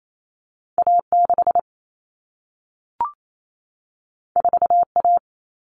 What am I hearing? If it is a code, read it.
Morse code, heard as a keyed tone, about 28 wpm, 700 Hz. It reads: A6 4A
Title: All Cut Numbers - 2-Digit 28wpm